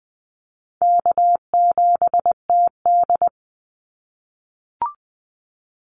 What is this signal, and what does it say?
Morse code, heard as a keyed tone, about 20 wpm, 700 Hz. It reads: K7TD
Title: Call Signs - 20wpm